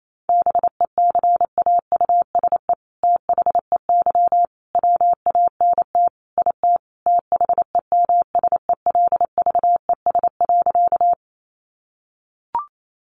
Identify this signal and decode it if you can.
Morse code, heard as a keyed tone, about 28 wpm, 700 Hz. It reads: BECAUSE THEY WANT IT THEMSELVES.
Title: Sentences from Top 800 Words 28wpm